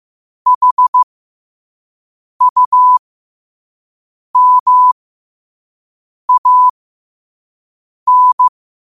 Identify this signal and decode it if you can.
Morse code, heard as a keyed tone, about 15 wpm, spaced as for 5 wpm, 1000 Hz. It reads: HUMAN
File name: qotdfiles / 5wpm